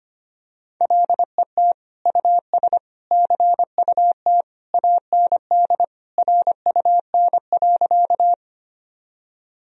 Morse code, heard as a keyed tone, about 25 wpm, 700 Hz. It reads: LET US CUT AND RUN.
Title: Sentences from Top 1000 Words - Review Part 1 - 25wpm